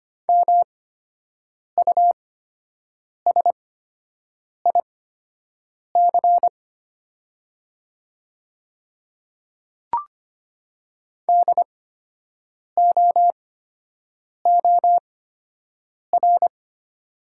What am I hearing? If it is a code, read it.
Morse code, heard as a keyed tone, about 25 wpm, spaced as for 7 wpm, 700 Hz. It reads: MUSIC DOOR
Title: 8x Character Spacing - Top 300 Words - 25wpm